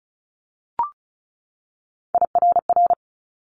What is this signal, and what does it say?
Morse code, heard as a keyed tone, about 36 wpm, 700 Hz. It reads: IRR
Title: Multi-letter Phonemes 35wpm